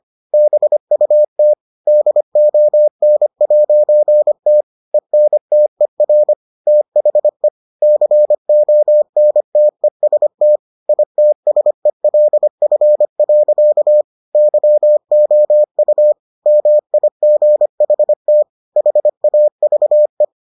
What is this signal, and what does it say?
Morse code, heard as a keyed tone, about 25 wpm, 600 Hz. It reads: BUT DON'T ENTER THE CONTEST ITSELF. YOU MIGHT HAVE